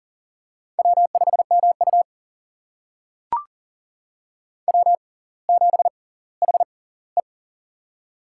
Morse code, heard as a keyed tone, about 39 wpm, spaced as for 20 wpm, 700 Hz. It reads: W5MU W7HE
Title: USA Call Signs - 4 Characters - 6x Character Spacing - 40wpm